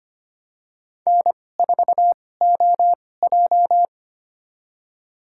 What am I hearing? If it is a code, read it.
Morse code, heard as a keyed tone, about 25 wpm, spaced as for 18 wpm, 700 Hz. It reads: N4OJ
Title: USA Call Signs - 4 Characters - 2x Character Spacing - 25wpm